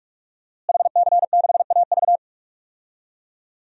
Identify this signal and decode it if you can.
Morse code, heard as a keyed tone, about 46 wpm, 700 Hz. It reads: HC6AV
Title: DXCC-100 Call Signs - 5 Characters - Repeated 2x - 45wpm